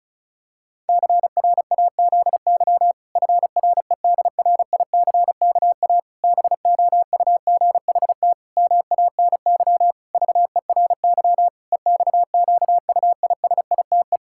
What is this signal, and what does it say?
Morse code, heard as a keyed tone, about 36 wpm, 700 Hz. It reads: CRAZY FREDRICKA BOUGHT MANY VERY EXQUISITE
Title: Pangram Challenge v2 - 35wpm